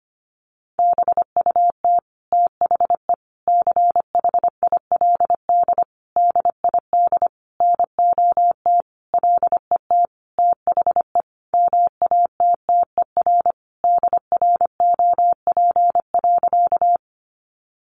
Morse code, heard as a keyed tone, about 25 wpm, 700 Hz. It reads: BUT THE CHILD DID NOT LET THE MATTER DROP.
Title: Sentences from Top 1400 Words 25wpm